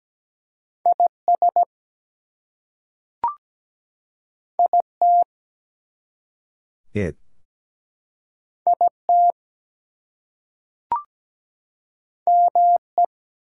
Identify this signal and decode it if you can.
Morse code, heard as a keyed tone, about 17 wpm, 700 Hz. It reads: IS IT IT ME
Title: Two Letter Words - 17wpm